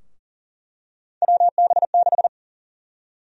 Morse code, heard as a keyed tone, about 40 wpm, 700 Hz. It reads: WB6
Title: USA Call Sign Prefixes - Repeated 2x - 40wpm